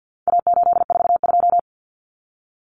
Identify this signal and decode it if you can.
Morse code, heard as a keyed tone, about 50 wpm, 700 Hz. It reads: A842
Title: Common Cut Numbers - 4-Digit 50wpm